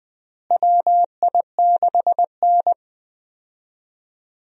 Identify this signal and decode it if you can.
Morse code, heard as a keyed tone, about 20 wpm, 700 Hz. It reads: WI6N
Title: USA Call Signs - 4 Characters - Repeated 3x - 20wpm